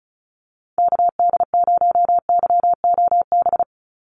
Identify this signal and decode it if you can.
Morse code, heard as a keyed tone, about 34 wpm, 700 Hz. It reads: KD0YOB
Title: NA SOTA Activator Callsigns 35wpm